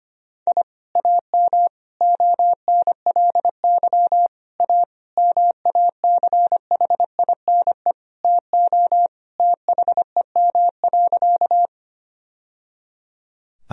Morse code, heard as a keyed tone, about 25 wpm, 700 Hz. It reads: I AM ONLY A MACHINE TO THEM.
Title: Sentences from Top 1000 Words - Review Part 2 - 25wpm